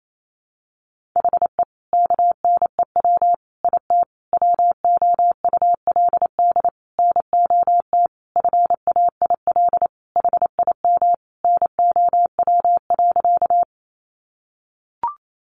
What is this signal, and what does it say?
Morse code, heard as a keyed tone, about 28 wpm, 700 Hz. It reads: HE KNEW IT WOULD NOT FAIL HIM NOW.
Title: Sentences from Top 1100 Words 28wpm